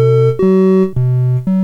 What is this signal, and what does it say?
There is no code.